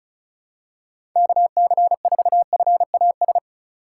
Morse code, heard as a keyed tone, about 34 wpm, 700 Hz. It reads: KC4FAS
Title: USA Call Signs - 4x Character Spacing - 35wpm